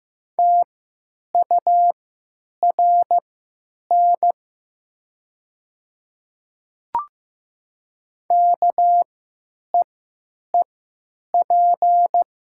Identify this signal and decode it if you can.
Morse code, heard as a keyed tone, about 15 wpm, spaced as for 8 wpm, 700 Hz. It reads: TURN KEEP
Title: 3x Character Spacing - Top 300 Words - 15wpm